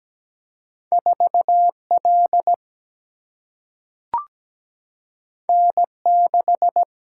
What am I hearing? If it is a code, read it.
Morse code, heard as a keyed tone, about 17 wpm, 700 Hz. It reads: 4L N6
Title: Single-Letter Single-Number 17wpm